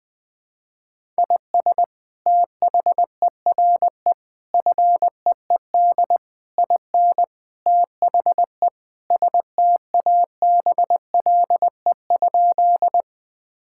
Morse code, heard as a keyed tone, about 20 wpm, 700 Hz. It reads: IS THERE FEED IN THE STABLE?
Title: Sentences from Top 1600 Words 20wpm